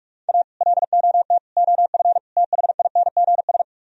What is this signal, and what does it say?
Morse code, heard as a keyed tone, about 58 wpm, 700 Hz. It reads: A LOT OF THINGS